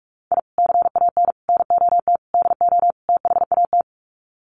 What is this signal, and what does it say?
Morse code, heard as a keyed tone, about 44 wpm, 700 Hz. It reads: I CAN NOT DO THAT
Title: Sets of 5 Words Encore2 45wpm